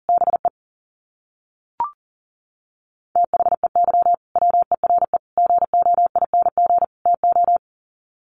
Morse code, heard as a keyed tone, about 40 wpm, 700 Hz. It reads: BE THEY WERE GOING TO
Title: Sets of 4 Words 40wpm